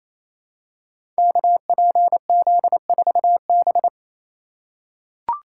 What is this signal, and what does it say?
Morse code, heard as a keyed tone, about 29 wpm, 690 Hz. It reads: KPZ4B
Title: Code Groups - 5 Letter:Numbers - 28wpm